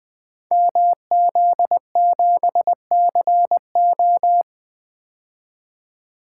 Morse code, heard as a keyed tone, about 20 wpm, 700 Hz. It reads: MZ7CO
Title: Code Groups - 5 Letter:Numbers - 20wpm